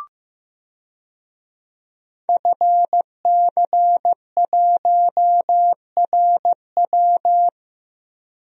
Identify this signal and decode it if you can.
Morse code, heard as a keyed tone, about 15 wpm, 700 Hz. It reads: FC1RW
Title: DXCC-100 Call Signs - 5 Characters - Repeated 3x - 15wpm